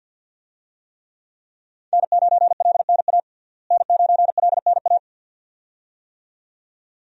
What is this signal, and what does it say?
Morse code, heard as a keyed tone, about 51 wpm, 700 Hz. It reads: N9LNA N9LNA
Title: USA Call Signs - 5 Characters - Repeated 2x - 50wpm